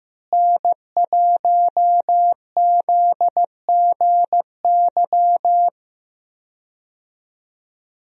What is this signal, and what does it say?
Morse code, heard as a keyed tone, about 15 wpm, 700 Hz. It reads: N1ZGY
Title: USA Call Signs - 5 Characters - Repeated 2x - 15wpm